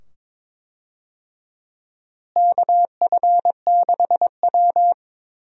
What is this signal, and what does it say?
Morse code, heard as a keyed tone, about 22 wpm, 700 Hz. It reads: KF6W
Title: USA Call Signs - 4 Characters - Repeated 2x - 22wpm